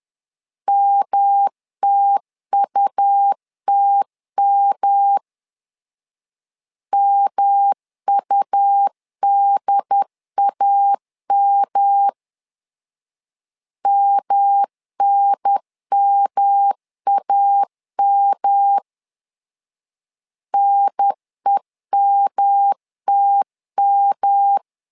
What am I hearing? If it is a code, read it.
Morse code, heard as a keyed tone, about 11 wpm, 800 Hz. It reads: MTUTM MUDAM MNMAM NEMTM
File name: LEZIONI / 3 MWRO / 3 M 10wpm 1828